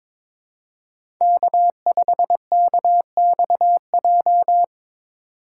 Morse code, heard as a keyed tone, about 22 wpm, 700 Hz. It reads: K5KXJ
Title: K1USN SST QSO Practice - 22wpm